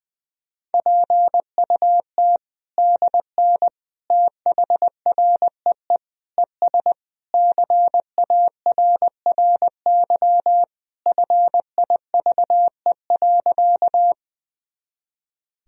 Morse code, heard as a keyed tone, about 20 wpm, 700 Hz. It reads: PUT DN THREE ES CARRY FIVE.
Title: Sentences from Top 400 Words and Abbreviations 20wpm